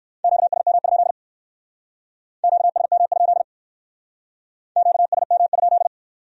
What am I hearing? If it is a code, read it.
Morse code, heard as a keyed tone, about 62 wpm, 700 Hz. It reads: QSM? QSM? QSM?